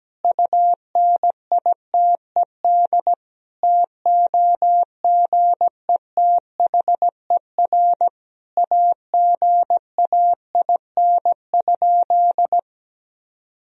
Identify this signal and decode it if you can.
Morse code, heard as a keyed tone, about 17 wpm, 700 Hz. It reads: UNITED TOGETHER AGAIN?
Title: Sentences from Top 1000 Words - Review Part 4 - 17wpm